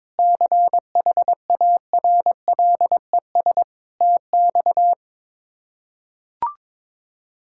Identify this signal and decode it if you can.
Morse code, heard as a keyed tone, about 21 wpm, 700 Hz. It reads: CHARLES TX
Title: K1USN SST QSO Practice - 22wpm